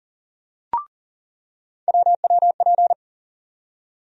Morse code, heard as a keyed tone, about 39 wpm, 700 Hz. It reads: WWP
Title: USA Call Sign Postfixes - 3 Letters 40wpm